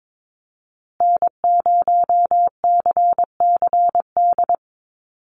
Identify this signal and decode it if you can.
Morse code, heard as a keyed tone, about 22 wpm, 700 Hz. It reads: N0CCD
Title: USA Call Signs Repeated 3x 22wpm